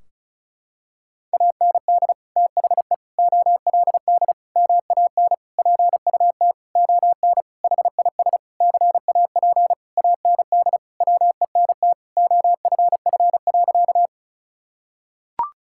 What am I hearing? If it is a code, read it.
Morse code, heard as a keyed tone, about 36 wpm, 700 Hz. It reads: AND THE OLD MAN PUT ON HIS CAP AND WENT OFF.
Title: Sentences from Top 1600 Words 35wpm